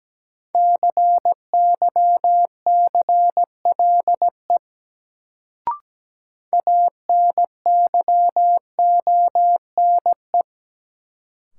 Morse code, heard as a keyed tone, about 17 wpm, 700 Hz. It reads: CYCLE ANYONE
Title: Sentences from Top 900 Words - New Word Drill - 17wpm